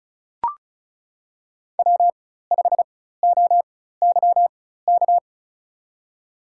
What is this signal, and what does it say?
Morse code, heard as a keyed tone, about 34 wpm, spaced as for 16 wpm, 700 Hz. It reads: W5OYK
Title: USA Call Signs - 5 Characters - 4x Character Spacing - 35wpm